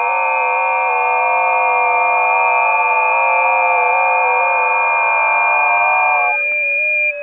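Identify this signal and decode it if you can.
FT8.
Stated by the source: FT8 recorded on a busy 20 m band.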